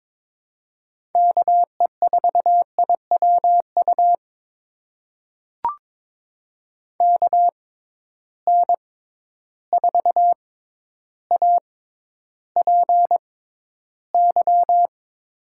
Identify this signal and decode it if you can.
Morse code, heard as a keyed tone, about 22 wpm, spaced as for 11 wpm, 700 Hz. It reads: KE4IWU KN4APY
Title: USA Call Signs - 6 Characters - 6x Character Spacing - 22wpm